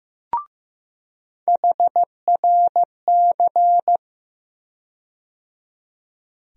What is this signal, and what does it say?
Morse code, heard as a keyed tone, about 15 wpm, 700 Hz. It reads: HRC
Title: USA Call Sign Postfixes - 3 Letters 15wpm